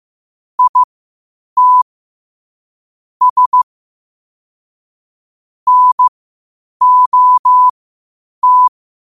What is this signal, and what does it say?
Morse code, heard as a keyed tone, about 15 wpm, spaced as for 8 wpm, 1000 Hz. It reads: IT S NOT
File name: qotdfiles / 8wpm